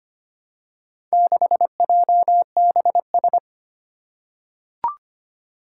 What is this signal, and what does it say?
Morse code, heard as a keyed tone, about 25 wpm, 700 Hz. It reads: 6JBS
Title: Four Challenging Letter-Number Combinations 25wpm